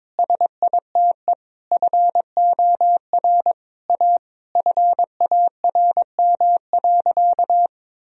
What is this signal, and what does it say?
Morse code, heard as a keyed tone, about 21 wpm, 700 Hz. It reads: SITE FOR A FARM.
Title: Sentences from Top 1000 Words - Review Part 4 - 22wpm